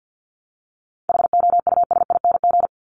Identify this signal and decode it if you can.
Morse code, heard as a keyed tone, about 50 wpm, 700 Hz. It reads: HOUSING
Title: Top 1001-1250 Words 50wpm